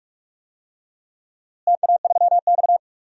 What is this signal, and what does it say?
Morse code, heard as a keyed tone, about 45 wpm, 700 Hz. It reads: TA3X
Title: DXCC-100 Call Signs - 4 Characters - Repeated 3x - 45wpm